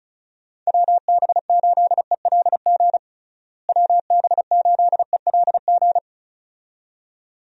Morse code, heard as a keyed tone, about 36 wpm, 700 Hz. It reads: WB8ELG WB8ELG
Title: USA Call Signs - 6 Characters - Repeated 2x - 35wpm